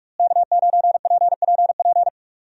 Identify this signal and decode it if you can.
Morse code, heard as a keyed tone, about 45 wpm, 700 Hz. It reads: K9PPP
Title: USA Call Signs - 5 Characters - Repeated 3x - 45wpm